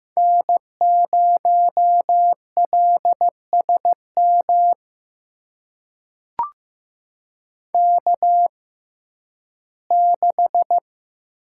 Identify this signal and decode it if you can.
Morse code, heard as a keyed tone, about 15 wpm, spaced as for 11 wpm, 700 Hz. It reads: N0LSM K6
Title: USA Call Signs - 6x Character Spacing - 15wpm